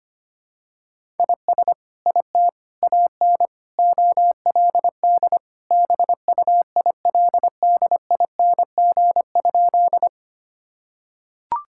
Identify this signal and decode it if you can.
Morse code, heard as a keyed tone, about 25 wpm, 700 Hz. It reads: IS IT AN OLD BUILDING?
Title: Sentences from Top 1000 Words - Review Part 5 - 25wpm